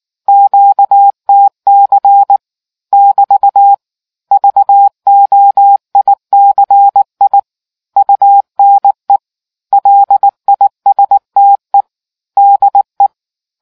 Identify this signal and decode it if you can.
Morse code, heard as a keyed tone, about 18 wpm, 800 Hz. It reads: QTC <BT> VOICI UNE LISTE DE